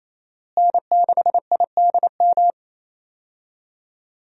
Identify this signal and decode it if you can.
Morse code, heard as a keyed tone, about 28 wpm, 700 Hz. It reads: N6IDM
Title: USA Call Signs - 5 Characters - Repeated 3x - 28wpm